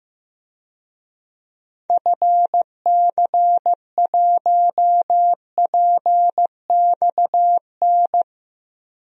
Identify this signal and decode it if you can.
Morse code, heard as a keyed tone, about 15 wpm, 700 Hz. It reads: FC1PXN
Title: DXCC-100 Call Signs - 6 Characters - 15wpm